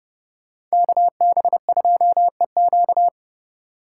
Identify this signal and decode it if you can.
Morse code, heard as a keyed tone, about 38 wpm, 700 Hz. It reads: KB2EQ